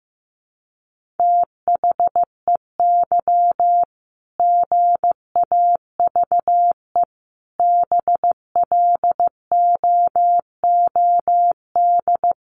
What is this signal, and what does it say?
Morse code, heard as a keyed tone, about 15 wpm, 700 Hz. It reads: THEY GAVE BLOOD
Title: Sentences from Top 1400 Words 15wpm